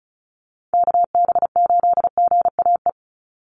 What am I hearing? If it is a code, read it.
Morse code, heard as a keyed tone, about 34 wpm, 700 Hz. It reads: KB8GAE
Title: Call Signs - 35wpm